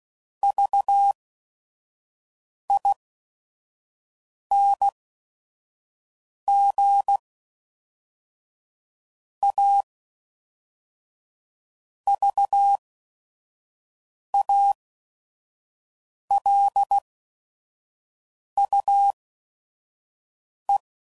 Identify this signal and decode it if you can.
Morse code, heard as a keyed tone, about 16 wpm, spaced as for 5 wpm, 800 Hz. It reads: VING A VALUE
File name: CW Code Groups @5WPM / Technical Subjects / XC